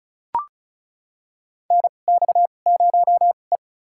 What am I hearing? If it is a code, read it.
Morse code, heard as a keyed tone, about 36 wpm, spaced as for 25 wpm, 700 Hz. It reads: NX0E
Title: USA Call Signs - 4 Characters - 2x Character Spacing - 35wpm